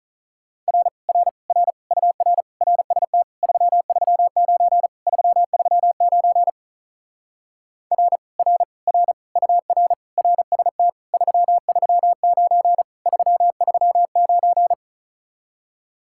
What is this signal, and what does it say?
Morse code, heard as a keyed tone, about 45 wpm, 700 Hz. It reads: R R R UR RST 339 339 R R R UR RST 339 339